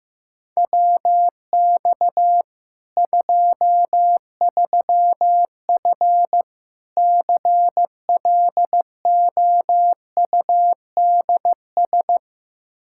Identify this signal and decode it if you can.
Morse code, heard as a keyed tone, about 15 wpm, 700 Hz. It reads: WX 23F CLOUDS
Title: Practice QSO - Exchange 2 - Info - 15wpm